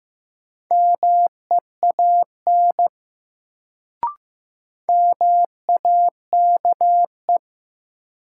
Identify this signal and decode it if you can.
Morse code, heard as a keyed tone, about 16 wpm, 700 Hz. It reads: MEAN MAKE
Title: Sets of 2 Words - New Word Drill - 15wpm